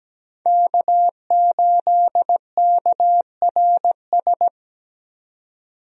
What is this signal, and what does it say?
Morse code, heard as a keyed tone, about 17 wpm, 700 Hz. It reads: K8KRS
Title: USA Call Signs - 5 Characters - 17wpm